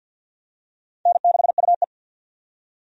Morse code, heard as a keyed tone, about 46 wpm, 700 Hz. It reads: N6UE
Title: USA Call Signs - 4 Characters - 50wpm